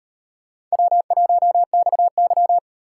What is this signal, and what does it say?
Morse code, heard as a keyed tone, about 38 wpm, 700 Hz. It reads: W1XY